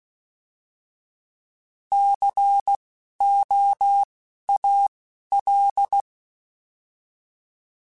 Morse code, heard as a keyed tone, about 16 wpm, spaced as for 12 wpm, 800 Hz. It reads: COAL
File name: Alphabetical / Alphabetical mp3s / Cs @10